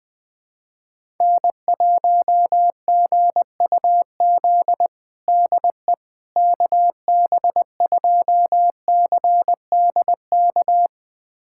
Morse code, heard as a keyed tone, about 20 wpm, 690 Hz. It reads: N1GUZ DE KB2CDK
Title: USA Callsign DE Callsign 20wpm